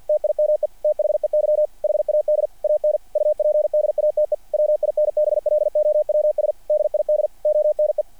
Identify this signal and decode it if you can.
Morse code, heard as a keyed tone, about 48 wpm, 600 Hz. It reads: TIME THEY HAD AN UPDATE WINBLOWS DID ONE
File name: archived / CRSnet